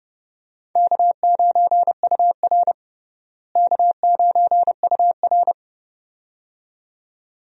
Morse code, heard as a keyed tone, about 30 wpm, 700 Hz. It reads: K9UR K9UR
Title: USA Call Signs - 4 Characters - Repeated 2x - 30wpm